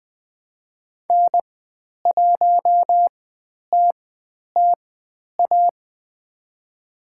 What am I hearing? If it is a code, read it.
Morse code, heard as a keyed tone, about 20 wpm, spaced as for 10 wpm, 700 Hz. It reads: N1TTA
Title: 095 Lesson 16g - Letter c - Call Signs 20wpm